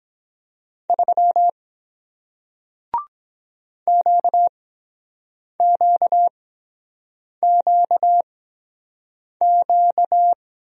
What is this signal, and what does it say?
Morse code, heard as a keyed tone, about 23 wpm, 700 Hz. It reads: 3 Q Q Q Q